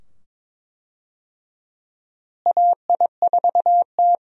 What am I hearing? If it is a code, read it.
Morse code, heard as a keyed tone, about 21 wpm, 700 Hz. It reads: AI4T
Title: USA Call Signs - 4 Characters - Repeated 2x - 22wpm